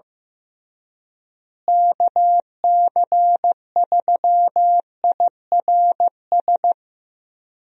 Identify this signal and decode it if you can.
Morse code, heard as a keyed tone, about 15 wpm, 700 Hz. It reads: KC3IRS
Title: USA Call Signs - 6 Characters - Repeated 3x - 15wpm